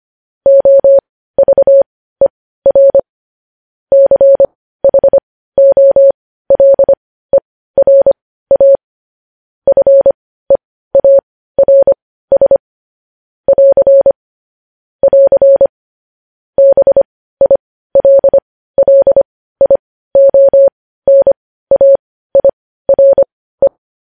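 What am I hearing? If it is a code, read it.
Morse code, heard as a keyed tone, about 25 wpm, spaced as for 15 wpm, 550 Hz. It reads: OVER CHOLERA FEARS <AR> <AR> BILLIONAIRE
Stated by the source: Morse code news headlines at 15.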